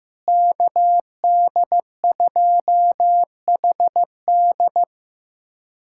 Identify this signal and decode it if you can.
Morse code, heard as a keyed tone, about 15 wpm, 700 Hz. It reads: KD2HD
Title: USA Callsign DE Callsign - 5 Characters - 15wpm